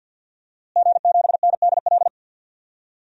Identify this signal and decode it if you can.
Morse code, heard as a keyed tone, about 52 wpm, 700 Hz. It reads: G7NDL
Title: DXCC-100 Call Signs - 5 Characters - Repeated 3x - 50wpm